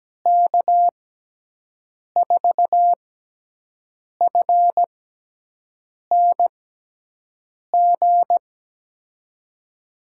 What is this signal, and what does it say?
Morse code, heard as a keyed tone, about 17 wpm, spaced as for 6 wpm, 700 Hz. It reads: K4FNG
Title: USA Call Signs - 6x Character Spacing - 17wpm